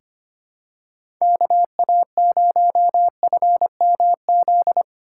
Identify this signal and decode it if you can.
Morse code, heard as a keyed tone, about 25 wpm, 700 Hz. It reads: KA0FMZ
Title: USA Call Signs - 6 Characters - 8x Character Spacing - 25wpm